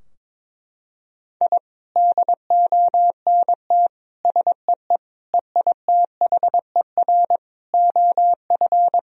Morse code, heard as a keyed tone, about 22 wpm, 700 Hz. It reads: I DONT SEE EITHER OF
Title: Sentences from Top 1000 Words - Review Part 4 - 22wpm